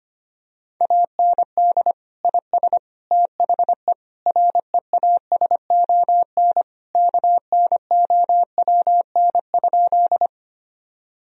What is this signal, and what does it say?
Morse code, heard as a keyed tone, about 25 wpm, 700 Hz. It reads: AND IS THE REASON KNOWN?
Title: Sentences from Top 1000 Words - Review Part 3 - 25wpm